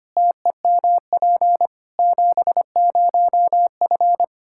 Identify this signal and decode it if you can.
Morse code, heard as a keyed tone, about 25 wpm, 700 Hz. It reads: TEMP 70F
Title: Practice QSO - Exchange 2 - Info - 25wpm